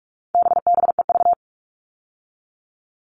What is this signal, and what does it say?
Morse code, heard as a keyed tone, about 44 wpm, 700 Hz. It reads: BBEV
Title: All Cut Numbers - 4-Digit 45wpm